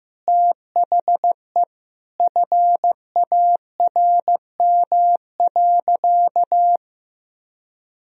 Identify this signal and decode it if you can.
Morse code, heard as a keyed tone, about 15 wpm, 700 Hz. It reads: THE FARM.
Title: Sentences from Top 1000 Words - Review Part 2 - 15wpm